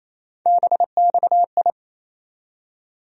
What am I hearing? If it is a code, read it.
Morse code, heard as a keyed tone, about 28 wpm, 700 Hz. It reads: BXI
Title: USA Call Sign Postfixes - 3 Letters 28wpm